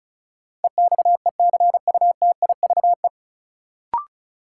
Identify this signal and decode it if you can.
Morse code, heard as a keyed tone, about 36 wpm, 700 Hz. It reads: EXECUTIVE
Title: Sets of 5 Words Encore - New Word Drill - 35wpm